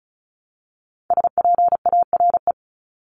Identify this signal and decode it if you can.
Morse code, heard as a keyed tone, about 34 wpm, 700 Hz. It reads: SPARE
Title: Top 1751-2000 Words 35wpm